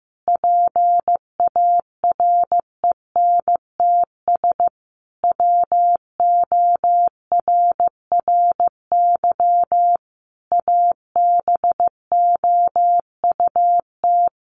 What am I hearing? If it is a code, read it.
Morse code, heard as a keyed tone, about 15 wpm, 700 Hz. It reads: PARENTS WORRY ABOUT
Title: Easy Phrases 15wpm